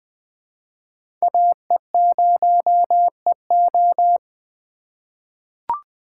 Morse code, heard as a keyed tone, about 20 wpm, 700 Hz. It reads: AE0EO
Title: USA Call Signs - 5 Characters - 3x Character Spacing - 20wpm